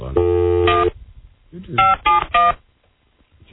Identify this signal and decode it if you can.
DTMF, heard as telephone touch tones, tones of about 180 ms. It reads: 1 571